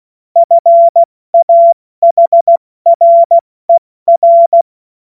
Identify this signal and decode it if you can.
Morse code, heard as a keyed tone, about 16 wpm, 680 Hz. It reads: FAHRER
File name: Morsekurs / Witze / Witz 03 Juni Mai 2025 70 F80